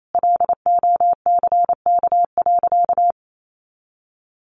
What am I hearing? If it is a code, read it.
Morse code, heard as a keyed tone, about 28 wpm, 700 Hz. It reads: LOCK.
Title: Sentences from Top 1300 Words 28wpm